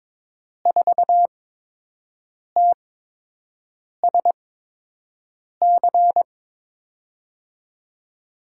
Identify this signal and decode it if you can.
Morse code, heard as a keyed tone, about 22 wpm, spaced as for 6 wpm, 700 Hz. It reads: 4TSC
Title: USA Call Signs - 6 Characters - 8x Character Spacing - 22wpm